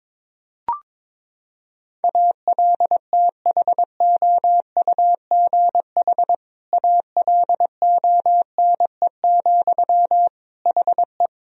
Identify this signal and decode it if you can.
Morse code, heard as a keyed tone, about 22 wpm, 700 Hz. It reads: ALTHOUGH ALONE, HE
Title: Sentences from Top 1000 Words - Review Part 3 - 22wpm